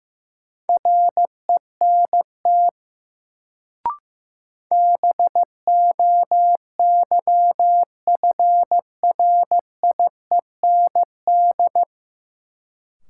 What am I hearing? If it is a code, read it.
Morse code, heard as a keyed tone, about 15 wpm, 700 Hz. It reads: RENT BOYFRIEND
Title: Sentences from Top 1000 Words - Review Part 5 - New Word Drill - 15wpm